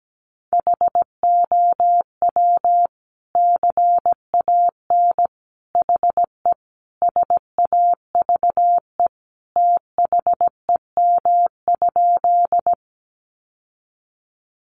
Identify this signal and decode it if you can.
Morse code, heard as a keyed tone, about 17 wpm, 700 Hz. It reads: HOW CAN HE SAVE THEM?
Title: Sentences from Top 1100 Words 17wpm